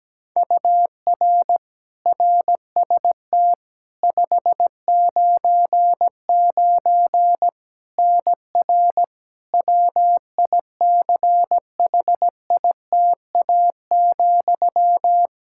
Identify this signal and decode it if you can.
Morse code, heard as a keyed tone, about 17 wpm, 700 Hz. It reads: UR RST 599 NR WICHITA,
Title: K1USN SST QSO Practice 17wpm